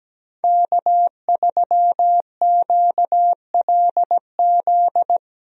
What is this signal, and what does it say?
Morse code, heard as a keyed tone, about 17 wpm, 700 Hz. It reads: K3QLZ
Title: USA Call Signs - 5 Characters - Repeated 2x - 17wpm